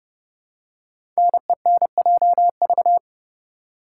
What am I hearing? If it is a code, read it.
Morse code, heard as a keyed tone, about 31 wpm, 690 Hz. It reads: NENJV
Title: Code Groups - 5 Letters - 30wpm